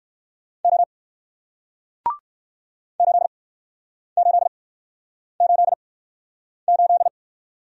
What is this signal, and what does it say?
Morse code, heard as a keyed tone, about 62 wpm, spaced as for 9 wpm, 700 Hz. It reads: O 8888